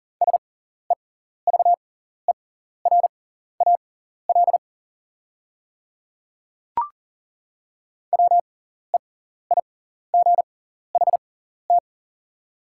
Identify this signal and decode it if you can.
Morse code, heard as a keyed tone, about 40 wpm, spaced as for 14 wpm, 690 Hz. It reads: SEVERAL WEIGHT
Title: 6x Character Spacing - Top 500 Words - 40wpm